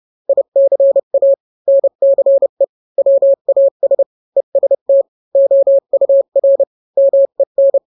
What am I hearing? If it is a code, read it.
Morse code, heard as a keyed tone, about 30 wpm, 550 Hz. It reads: ICA NCE WAS EST OUR MEN